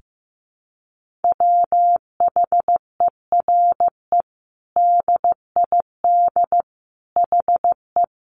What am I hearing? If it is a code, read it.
Morse code, heard as a keyed tone, about 15 wpm, 700 Hz. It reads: WHERE DID HE
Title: Sentences from Top 200 Words 15wpm